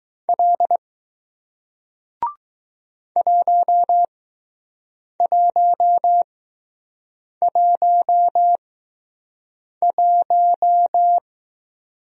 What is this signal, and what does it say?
Morse code, heard as a keyed tone, about 19 wpm, spaced as for 6 wpm, 700 Hz. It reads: L 1111